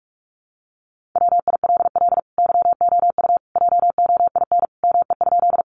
Morse code, heard as a keyed tone, about 44 wpm, 700 Hz. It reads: WILL YOU JOIN ME?
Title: Sentences from Top 1400 Words 45wpm